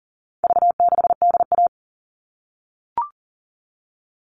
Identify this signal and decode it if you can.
Morse code, heard as a keyed tone, about 40 wpm, 700 Hz. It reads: V6DA
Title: All Cut Numbers - 4-Digit 40wpm